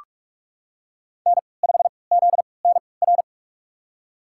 Morse code, heard as a keyed tone, about 45 wpm, spaced as for 25 wpm, 700 Hz. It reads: N5ZNR
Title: USA Call Signs - 5 Characters - 3x Character Spacing - 45wpm